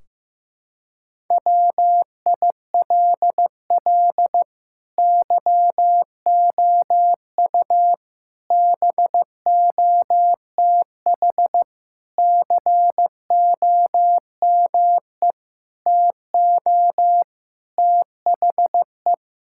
Morse code, heard as a keyed tone, about 15 wpm, 700 Hz. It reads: WILL YOU BOTH COME TO THE
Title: Sentences from Top 1600 Words 15wpm